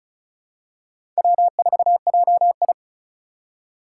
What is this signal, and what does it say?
Morse code, heard as a keyed tone, about 36 wpm, 700 Hz. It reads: W4JI
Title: USA Call Signs - 4 Characters - 4x Character Spacing - 35wpm